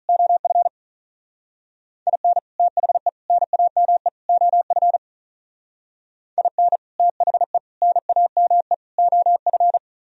Morse code, heard as a keyed tone, about 43 wpm, 700 Hz. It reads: OF IN THE NAME OF IN THE NAME OF
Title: Speed-Racing - Sets of 4 Words 35wpm